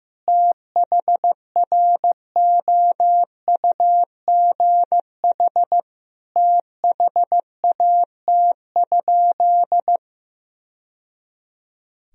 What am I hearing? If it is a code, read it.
Morse code, heard as a keyed tone, about 16 wpm, 700 Hz. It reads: THROUGH THAT?
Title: Sentences from Top 1000 Words - Review Part 1 - 15wpm